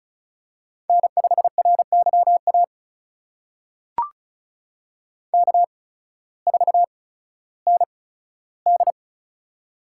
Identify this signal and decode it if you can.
Morse code, heard as a keyed tone, about 36 wpm, spaced as for 16 wpm, 700 Hz. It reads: N5RYA K4ND
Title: USA Call Signs - 8x Character Spacing - 35wpm